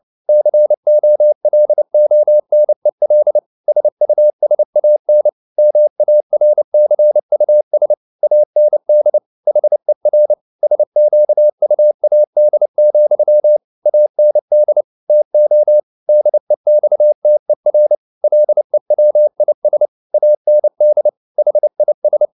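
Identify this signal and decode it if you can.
Morse code, heard as a keyed tone, about 29 wpm, 600 Hz. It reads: COLONEL SUSAN MARCUS AND HER SQUAD, AND TO DEXTER LEWIS AND HIS
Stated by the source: Morse code at 29 wpm, sending Ken Follett's novel Never in English, from CW med Gnister.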